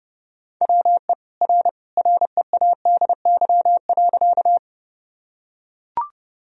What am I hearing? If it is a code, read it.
Morse code, heard as a keyed tone, about 30 wpm, 700 Hz. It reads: WE R READY.
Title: Sentences from Top 400 Words and Abbreviations 30wpm